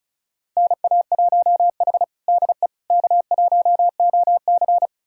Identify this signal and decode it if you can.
Morse code, heard as a keyed tone, about 36 wpm, 700 Hz. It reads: NA1H DE K1OC
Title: USA Callsign DE Callsign - 4 Characters - 35wpm